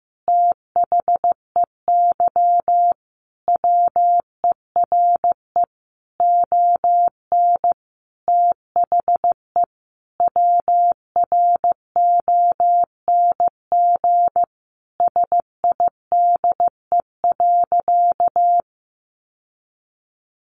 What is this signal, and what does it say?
Morse code, heard as a keyed tone, about 15 wpm, 700 Hz. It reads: THEY WERE ON THE WRONG SIDE.
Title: Sentences from Top 900 Words 15wpm